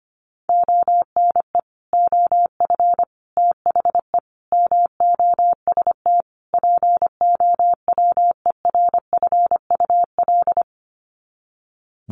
Morse code, heard as a keyed tone, about 25 wpm, 700 Hz. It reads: ONE OF THE MOST POWERFUL
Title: Sets of 5 Words Encore 25wpm